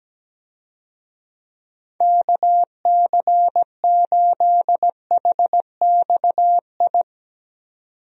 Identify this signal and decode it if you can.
Morse code, heard as a keyed tone, about 17 wpm, 700 Hz. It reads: KC8HXI
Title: USA Call Signs - 6 Characters - 17wpm